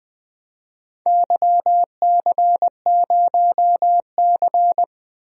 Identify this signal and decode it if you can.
Morse code, heard as a keyed tone, about 21 wpm, 700 Hz. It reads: YC0C
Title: DXCC-100 Call Signs - 4 Characters - 20wpm